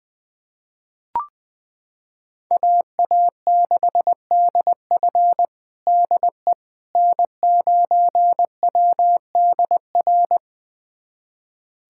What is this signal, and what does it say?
Morse code, heard as a keyed tone, about 21 wpm, 700 Hz. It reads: AA6DF DE N9WDR
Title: USA Callsign DE Callsign - 5 Characters - 20wpm